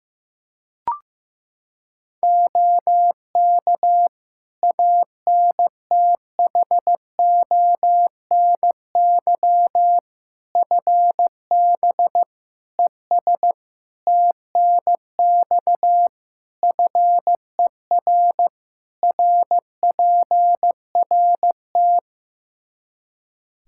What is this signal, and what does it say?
Morse code, heard as a keyed tone, about 15 wpm, 700 Hz. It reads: OK ANTHONY FB ES TNX FER RPRT
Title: Practice QSO - Exchange 2 - Info - 15wpm